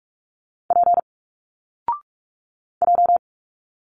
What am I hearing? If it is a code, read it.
Morse code, heard as a keyed tone, about 42 wpm, 700 Hz. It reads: P J
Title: Single Letters 45wpm